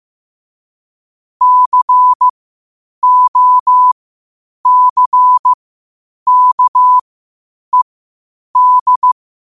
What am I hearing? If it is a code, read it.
Morse code, heard as a keyed tone, about 15 wpm, spaced as for 8 wpm, 1000 Hz. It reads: COCKED
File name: qotdfiles / 8wpm